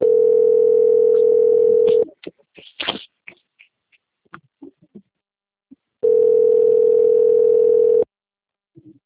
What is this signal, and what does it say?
Ringback tone (North America), 440 + 480 Hz.
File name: out- - - - - .36322